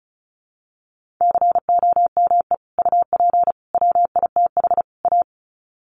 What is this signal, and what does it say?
Morse code, heard as a keyed tone, about 34 wpm, 700 Hz. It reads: COME UP WITH A
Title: Sets of 4 Words 35wpm